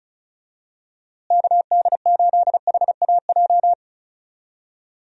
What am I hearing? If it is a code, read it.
Morse code, heard as a keyed tone, about 36 wpm, 700 Hz. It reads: KD8HAJ
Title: USA Call Signs - 6 Characters - Repeated 3x - 35wpm